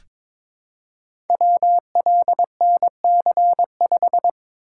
Morse code, heard as a keyed tone, about 22 wpm, 700 Hz. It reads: WLNC5
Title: Code Groups - 5 Letter:Numbers - 22wpm